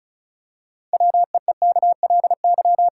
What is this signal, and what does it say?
Morse code, heard as a keyed tone, about 34 wpm, 700 Hz. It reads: WEEKLY
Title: Pangram Challenge v2 - 35wpm